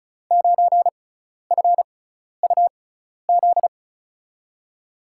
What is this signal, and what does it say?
Morse code, heard as a keyed tone, about 36 wpm, spaced as for 12 wpm, 700 Hz. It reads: 9FUZ
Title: USA Call Signs - 5 Characters - 6x Character Spacing - 35wpm